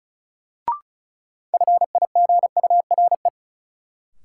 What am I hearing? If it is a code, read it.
Morse code, heard as a keyed tone, about 36 wpm, 700 Hz. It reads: FIGURE
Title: Sets of 4 Words - New Word Drill - 35wpm